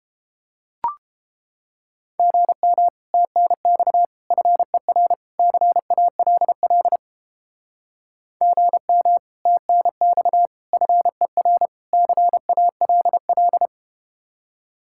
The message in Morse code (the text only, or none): GM TNX FER CALL GM TNX FER CALL